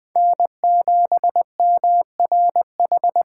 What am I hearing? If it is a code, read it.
Morse code, heard as a keyed tone, about 19 wpm, 700 Hz. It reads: N7MRH